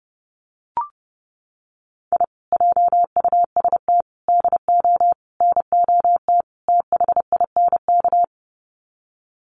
Morse code, heard as a keyed tone, about 28 wpm, 700 Hz. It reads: I JUST DO NOT THINK
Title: Sets of 5 Words Encore 30wpm